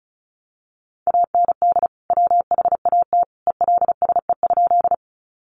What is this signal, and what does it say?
Morse code, heard as a keyed tone, about 34 wpm, 700 Hz. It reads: AND WHAT ELSE?
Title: Sentences from Top 800 Words 35wpm